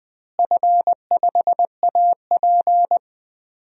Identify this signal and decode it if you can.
Morse code, heard as a keyed tone, about 20 wpm, 700 Hz. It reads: F5AP
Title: DXCC-100 Call Signs - 4 Characters - Repeated 3x - 20wpm